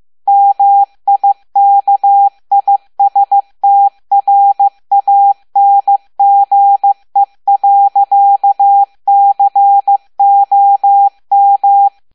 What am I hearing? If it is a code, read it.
Morse code, heard as a keyed tone, about 15 wpm, 800 Hz. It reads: MIKISTRANGE.COM